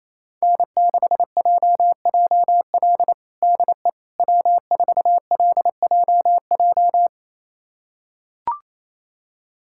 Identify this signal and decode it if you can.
Morse code, heard as a keyed tone, about 28 wpm, 700 Hz. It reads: N6JJL DE W4LJJ
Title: USA Callsign DE Callsign - 5 Characters - 28wpm